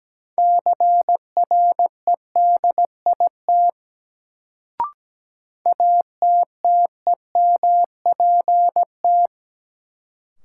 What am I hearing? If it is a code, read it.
Morse code, heard as a keyed tone, about 17 wpm, 700 Hz. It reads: CREDIT ATTEMPT
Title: Sentences from Top 1000 Words - New Word Drill - 17wpm